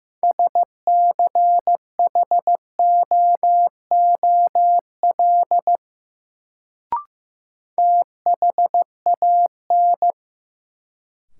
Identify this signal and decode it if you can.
Morse code, heard as a keyed tone, about 15 wpm, 700 Hz. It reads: SCHOOL THAN
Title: Sets of 2 Words - New Word Drill - 15wpm